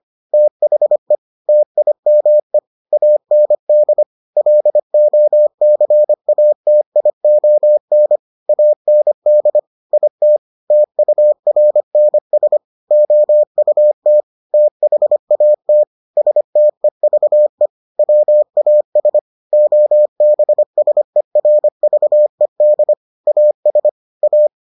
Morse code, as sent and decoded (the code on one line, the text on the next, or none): - .... / - .. -- . / .- -. -.. / .-.. --- -.-. .- - .. --- -. / .- -. -.. / .. - / - ..- .-. -. ... / --- ..- - / - .... .- - / ... - . ...- . / .-- .- ... / --- -... ... .-. ...- . -.. / .- ... / .-
THE TIME AND LOCATION AND IT TURNS OUT THAT STEVE WAS OBSERVED AS A